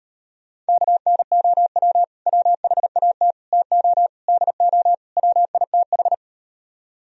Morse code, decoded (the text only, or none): KNOW WHAT TO DO WITH